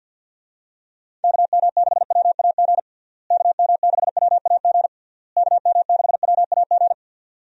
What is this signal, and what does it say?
Morse code, heard as a keyed tone, about 51 wpm, 700 Hz. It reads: KM6WAG KM6WAG KM6WAG
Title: USA Call Signs - 6 Characters - Repeated 3x - 50wpm